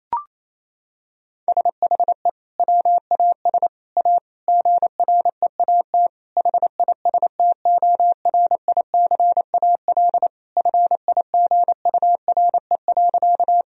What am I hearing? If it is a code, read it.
Morse code, heard as a keyed tone, about 28 wpm, 700 Hz. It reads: SHE WAS A GREAT HISTORICAL FIGURE.
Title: Sentences from Top 1000 Words - Review Part 4 - 28wpm